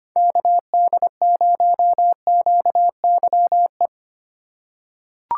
Morse code, heard as a keyed tone, about 25 wpm, 700 Hz. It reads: KD0QYE
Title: USA Call Signs - 6 Characters - 4x Character Spacing - 25wpm